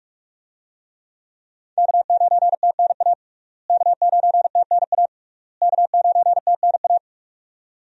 Morse code, heard as a keyed tone, about 42 wpm, 700 Hz. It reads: K9TNA K9TNA K9TNA
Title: USA Call Signs - 5 Characters - Repeated 3x - 45wpm